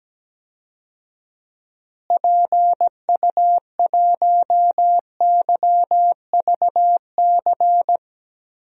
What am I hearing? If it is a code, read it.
Morse code, heard as a keyed tone, about 17 wpm, 700 Hz. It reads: PU1YVC
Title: DXCC-100 Call Signs - 6 Characters - Repeated 3x - 17wpm